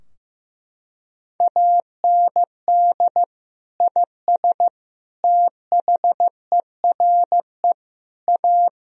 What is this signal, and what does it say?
Morse code, heard as a keyed tone, about 15 wpm, 700 Hz. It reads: AND IS THERE A
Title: Sentences from Top 1000 Words - Review Part 3 - 15wpm